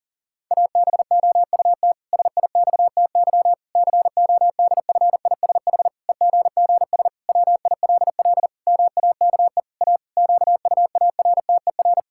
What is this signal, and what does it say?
Morse code, heard as a keyed tone, about 40 wpm, 700 Hz. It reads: ABOUT SIXTY CODFISH EGGS WILL MAKE A QUARTER
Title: Pangram Challenge v2 - 40wpm